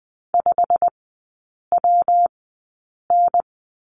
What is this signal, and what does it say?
Morse code, heard as a keyed tone, about 20 wpm, spaced as for 8 wpm, 700 Hz. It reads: 5WN
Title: 172 Lesson 28e - Number 9 - Three Characters 20wpm